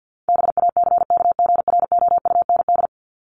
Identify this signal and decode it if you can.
Morse code, heard as a keyed tone, about 50 wpm, 700 Hz. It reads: BACKGROUND
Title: Top 1001-1250 Words 50wpm